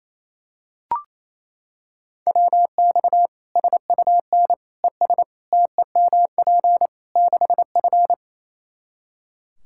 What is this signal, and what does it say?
Morse code, heard as a keyed tone, about 28 wpm, 700 Hz. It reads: WX SUN ES TEMP 6F
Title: Practice QSO - Exchange 2 - Info - 28wpm